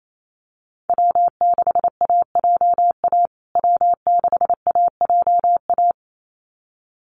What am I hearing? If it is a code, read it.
Morse code, heard as a keyed tone, about 28 wpm, 700 Hz. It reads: W6AJA W6AJA
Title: USA Call Signs Repeated 2x 28wpm